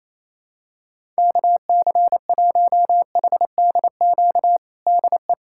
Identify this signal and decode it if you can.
Morse code, heard as a keyed tone, about 27 wpm, 700 Hz. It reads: KC1HDQ DE
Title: USA Callsign DE Callsign - 6 Characters - 28wpm